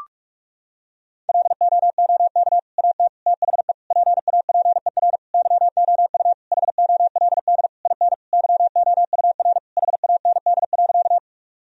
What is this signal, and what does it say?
Morse code, heard as a keyed tone, about 46 wpm, 700 Hz. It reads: LOOK AT THE PAPER YOU HOLD IN YOUR HAND.
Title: Sentences from Top 1000 Words - Review Part 2 - 45wpm